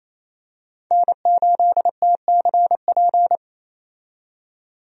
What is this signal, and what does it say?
Morse code, heard as a keyed tone, about 28 wpm, 700 Hz. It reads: N8TCP
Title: USA Call Signs - 5 Characters - 28wpm